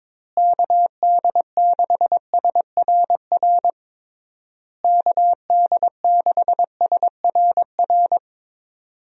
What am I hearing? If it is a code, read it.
Morse code, heard as a keyed tone, about 22 wpm, 700 Hz. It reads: KD6SRR KD6SRR
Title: USA Call Signs - 6 Characters - Repeated 3x - 22wpm